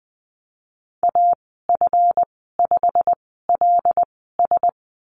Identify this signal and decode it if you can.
Morse code, heard as a keyed tone, about 20 wpm, spaced as for 15 wpm, 700 Hz. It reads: AF5LS
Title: 231 Lesson 37g - Letter x - Call Signs 20wpm